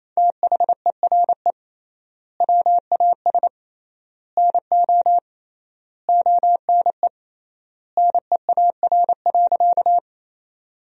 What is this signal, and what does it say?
Morse code, heard as a keyed tone, about 28 wpm, 700 Hz. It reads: THERE WAS NO ONE NEAR.
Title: 3x Word-Spacing - Sentences from Top 200 Words - 28wpm